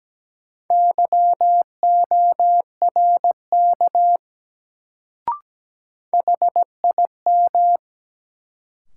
Morse code, heard as a keyed tone, about 17 wpm, 700 Hz. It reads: YORK HIM
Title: Sets of 2 Words - New Word Drill - 17wpm